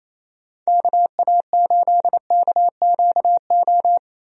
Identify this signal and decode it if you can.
Morse code, heard as a keyed tone, about 28 wpm, 700 Hz. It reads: KA8KQO
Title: USA Call Signs - 6 Characters - 28wpm